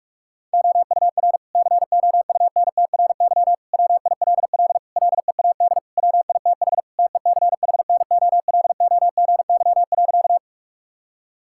Morse code, heard as a keyed tone, about 42 wpm, 700 Hz. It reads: OUR COUNTRY WILL LEAD WITH TECHNOLOGY.
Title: Sentences from Top 1000 Words - Review Part 5 - 45wpm